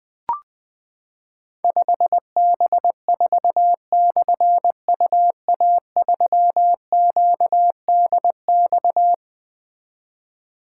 Morse code, heard as a keyed tone, about 20 wpm, 700 Hz. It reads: 5B4/UA3QDX
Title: Difficult Call Signs 20wpm